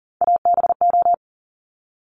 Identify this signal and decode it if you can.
Morse code, heard as a keyed tone, about 38 wpm, 690 Hz. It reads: ABO
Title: Top Three Letter Combinations in Words 40wpm